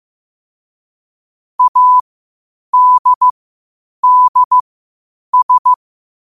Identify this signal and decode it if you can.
Morse code, heard as a keyed tone, about 15 wpm, spaced as for 8 wpm, 1000 Hz. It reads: ADDS
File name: qotdfiles / 8wpm